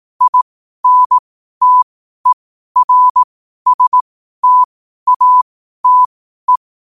Morse code, heard as a keyed tone, about 18 wpm, spaced as for 12 wpm, 1000 Hz. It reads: INTERSTATE
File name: qotdfiles / 12wpm